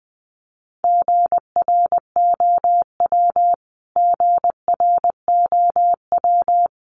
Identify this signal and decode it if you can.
Morse code, heard as a keyed tone, about 20 wpm, 700 Hz. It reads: GROW GROW
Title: Top 500 Words Repeated 20wpm